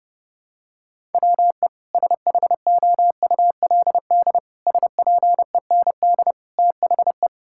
Morse code, heard as a keyed tone, about 30 wpm, 700 Hz. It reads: WE SHOULD SPEND THE
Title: Sentences from Top 1000 Words - Review Part 5 - 30wpm